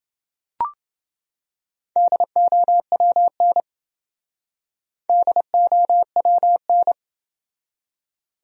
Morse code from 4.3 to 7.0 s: DOWN

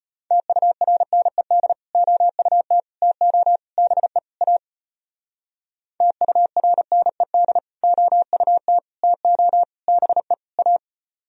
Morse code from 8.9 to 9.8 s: TO

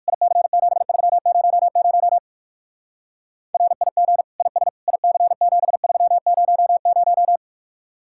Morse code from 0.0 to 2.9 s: IC7300

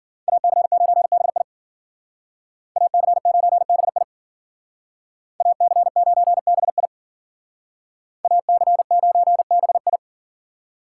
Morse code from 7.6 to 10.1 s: AC9BI

